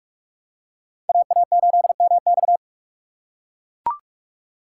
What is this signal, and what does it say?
Morse code, heard as a keyed tone, about 40 wpm, 700 Hz. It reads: AA8MX